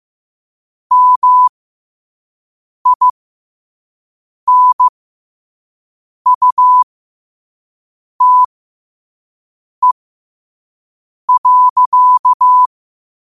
Morse code, heard as a keyed tone, about 15 wpm, spaced as for 5 wpm, 1000 Hz. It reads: MINUTE.
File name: qotdfiles / 5wpm